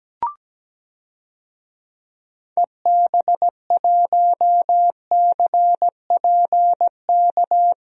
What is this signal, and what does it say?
Morse code, heard as a keyed tone, about 17 wpm, 700 Hz. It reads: EB1CPK
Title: DXCC-100 Call Signs - 6 Characters - Repeated 3x - 17wpm